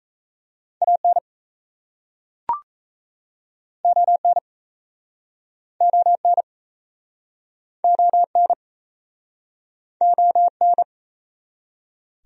Morse code, heard as a keyed tone, about 38 wpm, 700 Hz. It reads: AN ON ON ON ON